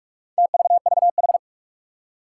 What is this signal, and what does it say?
Morse code, heard as a keyed tone, about 44 wpm, 700 Hz. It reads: TVVH
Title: Four Challenging Letter-Number Combinations 45wpm